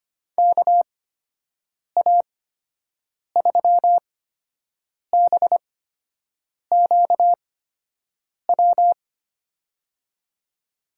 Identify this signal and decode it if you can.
Morse code, heard as a keyed tone, about 25 wpm, spaced as for 7 wpm, 700 Hz. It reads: KA3BQW
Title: USA Call Signs - 6 Characters - 8x Character Spacing - 25wpm